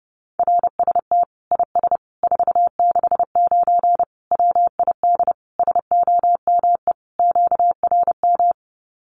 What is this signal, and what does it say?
Morse code, heard as a keyed tone, about 28 wpm, 700 Hz. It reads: RST IS 469 WID SOME QRM
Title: Common QSO Phrases 30wpm